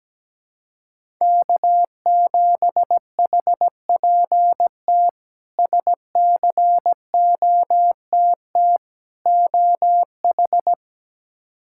Morse code, heard as a keyed tone, about 17 wpm, 700 Hz. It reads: K7HPT SCOTT OH
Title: K1USN SST QSO Practice - 17wpm